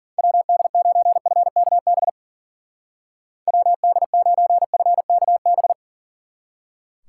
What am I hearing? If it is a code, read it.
Morse code, heard as a keyed tone, about 52 wpm, 700 Hz. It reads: WD9FKB WD9FKB